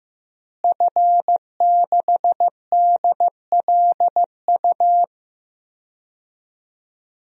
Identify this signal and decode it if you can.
Morse code, heard as a keyed tone, about 15 wpm, 700 Hz. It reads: F6DLU
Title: DXCC-100 Call Signs - 5 Characters - Repeated 2x - 15wpm